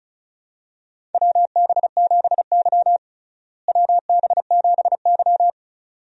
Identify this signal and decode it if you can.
Morse code, heard as a keyed tone, about 36 wpm, 700 Hz. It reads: WB7Y WB7Y
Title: USA Call Signs - 4 Characters - Repeated 2x - 35wpm